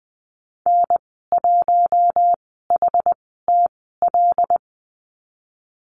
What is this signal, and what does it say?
Morse code, heard as a keyed tone, about 20 wpm, spaced as for 15 wpm, 700 Hz. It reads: N1HTL
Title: 122 Lesson 20g - Number 3 - Call Signs 20wpm